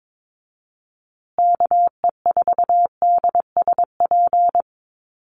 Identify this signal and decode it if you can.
Morse code, heard as a keyed tone, about 22 wpm, 700 Hz. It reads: KE4DSP
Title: USA Call Signs Repeated 3x 22wpm